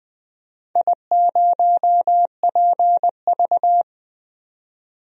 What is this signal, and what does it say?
Morse code, heard as a keyed tone, about 20 wpm, 700 Hz. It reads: I0PV
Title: DXCC-100 Call Signs - 4 Characters - 20wpm